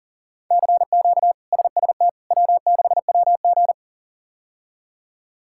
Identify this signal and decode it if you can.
Morse code, heard as a keyed tone, about 40 wpm, 700 Hz. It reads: CQ SST W6WG
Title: K1USN SST QSO Practice - 40wpm